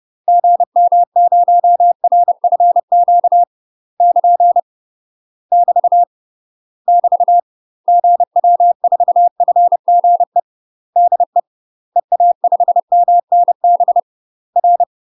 Morse code, heard as a keyed tone, about 30 wpm, 700 Hz. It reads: GM0RFQ <KN> <BT> <BT> GW4FGE DE EA5MNB R
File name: UK MASTER QSOs TRAINING / 30WPM / UK MASTER QSOs 0016